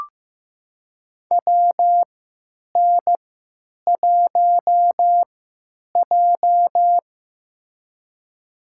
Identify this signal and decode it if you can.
Morse code, heard as a keyed tone, about 15 wpm, spaced as for 9 wpm, 700 Hz. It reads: WN1J
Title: USA Call Signs - 3x Character Spacing - 15wpm